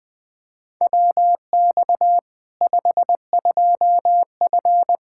Morse code, heard as a keyed tone, about 20 wpm, 700 Hz. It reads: WX 52F
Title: K1USN SST QSO Practice 20wpm